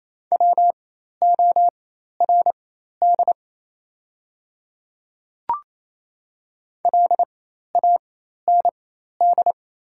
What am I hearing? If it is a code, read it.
Morse code, heard as a keyed tone, about 28 wpm, spaced as for 13 wpm, 690 Hz. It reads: WORD LAND
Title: 4x Character Spacing - Top 200 Words - 28wpm